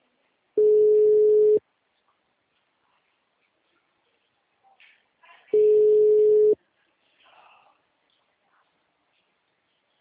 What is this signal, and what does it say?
Ringback tone (Europe, 425 Hz).
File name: out- -2017- - - .66953